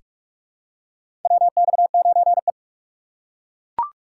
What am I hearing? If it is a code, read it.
Morse code, heard as a keyed tone, about 41 wpm, 700 Hz. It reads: WX9E
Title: USA Call Signs - 4 Characters - 4x Character Spacing - 45wpm